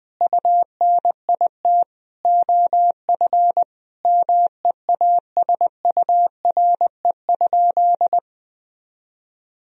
Morse code, heard as a keyed tone, about 20 wpm, 700 Hz. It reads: UNIT OF MEASURE?
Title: Sentences from Top 1000 Words - Review Part 2 - 20wpm